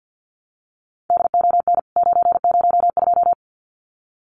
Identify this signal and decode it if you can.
Morse code, heard as a keyed tone, about 50 wpm, 700 Hz. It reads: DON 902
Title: CWT Exchange 50wpm